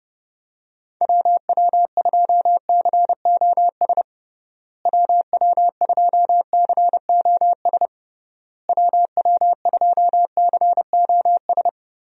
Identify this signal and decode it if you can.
Morse code, heard as a keyed tone, about 30 wpm, 700 Hz. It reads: WW2COS WW2COS WW2COS
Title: USA Call Signs - 6 Characters - Repeated 3x - 30wpm